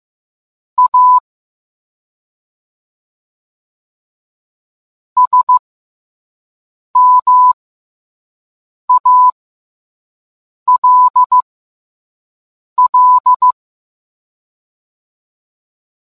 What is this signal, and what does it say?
Morse code, heard as a keyed tone, about 15 wpm, spaced as for 5 wpm, 1000 Hz. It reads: A SMALL